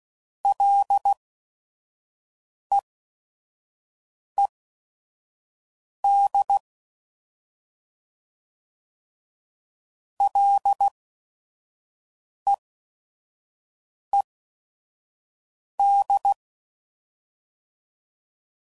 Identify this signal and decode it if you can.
Morse code, heard as a keyed tone, about 16 wpm, spaced as for 5 wpm, 800 Hz. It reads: LEED LEED
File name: Prefixes / FLs & THs / Single Fs Ls